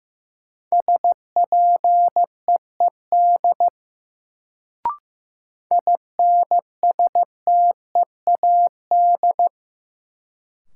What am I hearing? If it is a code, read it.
Morse code, heard as a keyed tone, about 16 wpm, 700 Hz. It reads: SPEED INSTEAD
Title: Sentences from Top 800 Words - New Word Drill - 15wpm